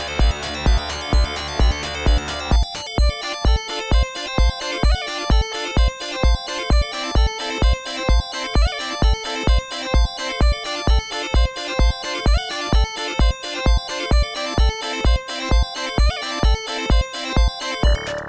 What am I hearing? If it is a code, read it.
There is no code.